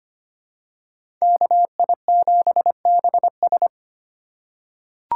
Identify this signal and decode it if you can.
Morse code, heard as a keyed tone, about 25 wpm, 700 Hz. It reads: KI7BS
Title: USA Call Signs - 5 Characters - 2x Character Spacing - 25wpm